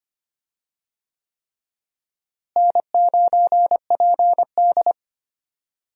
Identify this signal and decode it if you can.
Morse code, heard as a keyed tone, about 25 wpm, 700 Hz. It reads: N9PD
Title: USA Call Signs - 4 Characters - Repeated 3x - 25wpm